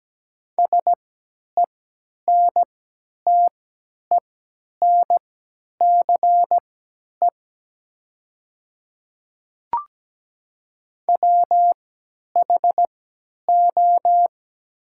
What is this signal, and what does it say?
Morse code, heard as a keyed tone, about 17 wpm, spaced as for 10 wpm, 700 Hz. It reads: SENTENCE WHO
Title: 3x Character Spacing - Top 200 Words - 17wpm